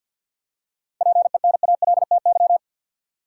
Morse code, heard as a keyed tone, about 52 wpm, 700 Hz. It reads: PENALTY
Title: Seven Letter Words - 50wpm